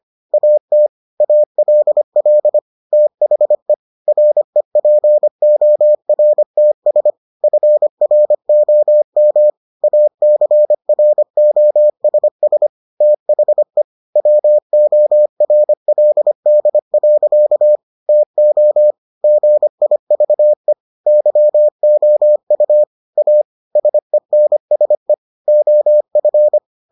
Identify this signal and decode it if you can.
Morse code, heard as a keyed tone, about 25 wpm, 600 Hz. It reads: AT ALL THE REPORTS FROM ACROSS THE WORLD. TO GIVE YOU A SENSE OF